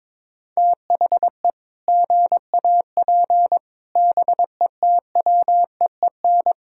Morse code, heard as a keyed tone, about 21 wpm, 700 Hz. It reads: THE GAP BETWEEN
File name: practice / 22wpm / Sets of 8 Words - 22wpm